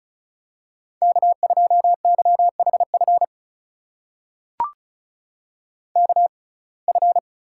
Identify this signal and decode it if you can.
Morse code, heard as a keyed tone, about 36 wpm, spaced as for 25 wpm, 700 Hz. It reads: K2YHF KF
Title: USA Call Signs - 5 Characters - 6x Character Spacing - 35wpm